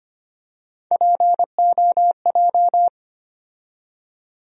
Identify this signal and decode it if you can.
Morse code, heard as a keyed tone, about 25 wpm, 700 Hz. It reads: POJ
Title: USA Call Sign Postfixes - 3 Letters 25wpm